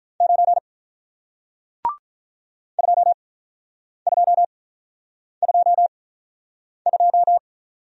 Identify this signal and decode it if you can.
Morse code, heard as a keyed tone, about 48 wpm, spaced as for 9 wpm, 700 Hz. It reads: 9 2222